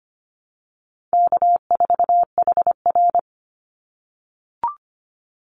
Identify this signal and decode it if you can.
Morse code, heard as a keyed tone, about 25 wpm, 700 Hz. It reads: K4HR
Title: Call Signs - 25wpm